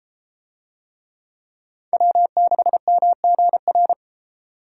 Morse code, heard as a keyed tone, about 31 wpm, 700 Hz. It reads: W6MGR